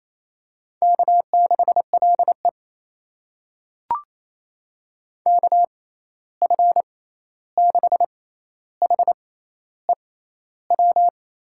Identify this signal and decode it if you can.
Morse code, heard as a keyed tone, about 28 wpm, spaced as for 13 wpm, 700 Hz. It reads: K6LE KF6HEW
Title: USA Call Signs - 6x Character Spacing - 28wpm